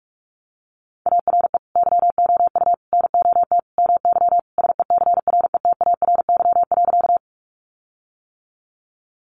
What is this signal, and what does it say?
Morse code, heard as a keyed tone, about 44 wpm, 700 Hz. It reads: ARE YOU NOT MY SECRETARY.
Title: Sentences from Top 1300 Words 45wpm